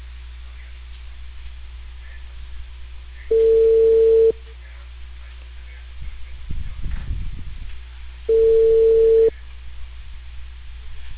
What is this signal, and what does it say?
Ringback tone (China, 450 Hz), with speech.